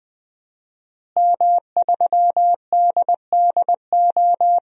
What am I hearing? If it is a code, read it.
Morse code, heard as a keyed tone, about 20 wpm, 700 Hz. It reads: M3DDO
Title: DXCC-100 Call Signs - 5 Characters - Repeated 3x - 20wpm